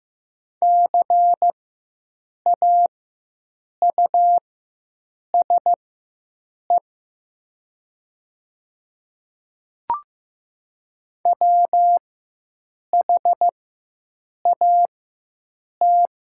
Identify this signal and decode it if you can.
Morse code, heard as a keyed tone, about 15 wpm, spaced as for 7 wpm, 700 Hz. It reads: CAUSE WHAT
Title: 4x Character Spacing - Top 300 Words - 15wpm